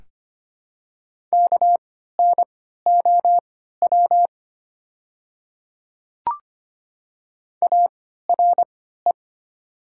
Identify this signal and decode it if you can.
Morse code, heard as a keyed tone, about 25 wpm, spaced as for 14 wpm, 700 Hz. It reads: KNOW ARE
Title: 3x Character Spacing - Top 200 Words - 25wpm